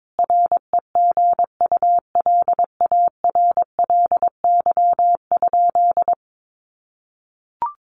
Morse code, heard as a keyed tone, about 22 wpm, 700 Hz. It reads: REGULARLY?
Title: Sentences from Top 1400 Words 22wpm